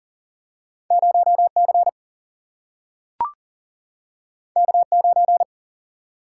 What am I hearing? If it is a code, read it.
Morse code, heard as a keyed tone, about 39 wpm, 700 Hz. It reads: 0 C K 9
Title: Single-Letter Single-Number 40wpm